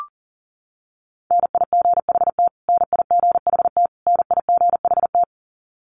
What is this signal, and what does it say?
Morse code, heard as a keyed tone, about 40 wpm, 700 Hz. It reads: NIGHT NIGHT NIGHT
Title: Top 300 Words Repeated 40wpm